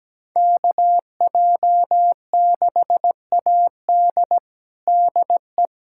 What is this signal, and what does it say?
Morse code, heard as a keyed tone, about 17 wpm, 700 Hz. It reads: KJ6AD DE
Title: USA Callsign DE Callsign - 5 Characters - 17wpm